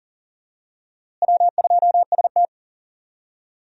Morse code, heard as a keyed tone, about 38 wpm, 700 Hz. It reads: W2ST
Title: USA Call Signs - 4 Characters - Repeated 2x - 40wpm